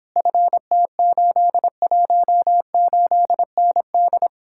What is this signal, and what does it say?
Morse code, heard as a keyed tone, about 29 wpm, 700 Hz. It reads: FT818ND